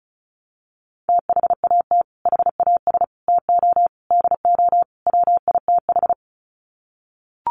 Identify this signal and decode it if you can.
Morse code, heard as a keyed tone, about 34 wpm, 700 Hz. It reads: THAT HAS TO DO WITH
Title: Sets of 5 Words Encore2 35wpm